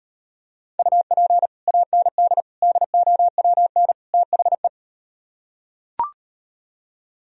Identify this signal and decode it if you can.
Morse code, heard as a keyed tone, about 38 wpm, 700 Hz. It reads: UP AND DOWN THE